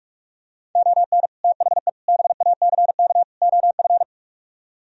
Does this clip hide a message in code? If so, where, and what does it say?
Yes: Morse code from 0.8 to 4.0 s, heard as a keyed tone, about 49 wpm, 700 Hz. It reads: ON THE BACK OF